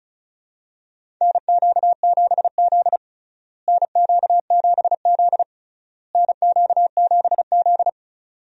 Morse code, heard as a keyed tone, about 36 wpm, 700 Hz. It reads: NQ7Z NQ7Z NQ7Z
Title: USA Call Signs - 4 Characters - Repeated 3x - 35wpm